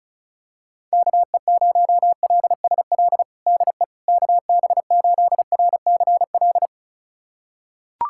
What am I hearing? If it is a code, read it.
Morse code, heard as a keyed tone, about 36 wpm, 700 Hz. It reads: KE0LSL DE KB8RCL
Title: USA Callsign DE Callsign 35wpm